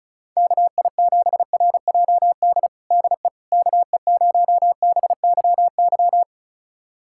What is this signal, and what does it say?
Morse code, heard as a keyed tone, about 34 wpm, 700 Hz. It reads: KI7RJD DE KE0BYY
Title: USA Callsign DE Callsign - 6 Characters - 35wpm